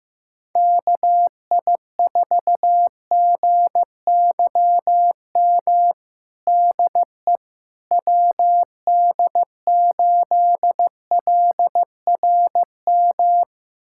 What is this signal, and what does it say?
Morse code, heard as a keyed tone, about 15 wpm, 700 Hz. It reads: KI4GYM DE WD8LRM
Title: USA Callsign DE Callsign - 6 Characters - 15wpm